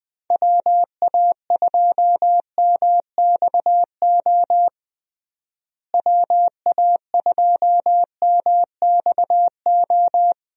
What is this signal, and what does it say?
Morse code, heard as a keyed tone, about 20 wpm, 700 Hz. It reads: WA2MXO WA2MXO
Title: USA Call Signs - 6 Characters - Repeated 2x - 20wpm